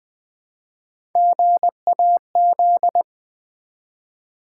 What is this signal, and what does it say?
Morse code, heard as a keyed tone, about 20 wpm, 700 Hz. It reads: GAZ
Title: Build A Word - 1 Letter At A Time - 3 to 5-Letter Words - 20wpm